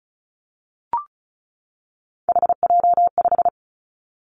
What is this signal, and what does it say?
Morse code, heard as a keyed tone, about 36 wpm, 700 Hz. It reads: HJ5
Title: Callsign Prefixes for Top 100 DXCC - 35wpm